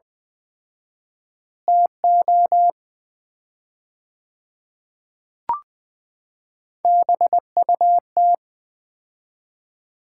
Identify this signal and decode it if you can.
Morse code, heard as a keyed tone, about 20 wpm, 690 Hz. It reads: TO BUT
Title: 4x Word-Spacing - Sets of 3 Words - 20wpm